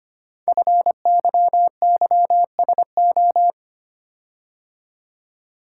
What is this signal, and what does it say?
Morse code, heard as a keyed tone, about 25 wpm, 700 Hz. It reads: FYYSO
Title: Code Groups - 5 Letters - 25wpm